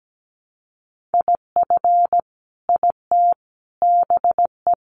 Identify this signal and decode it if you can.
Morse code, heard as a keyed tone, about 17 wpm, 700 Hz. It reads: IF IT BE
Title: Sentences from Top 1300 Words 17wpm